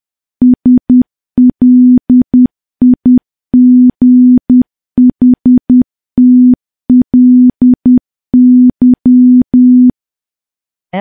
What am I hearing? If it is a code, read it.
Morse code, heard as a keyed tone, about 10 wpm, 250 Hz. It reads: SLIGHTLY
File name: morse / morse-training-10WPM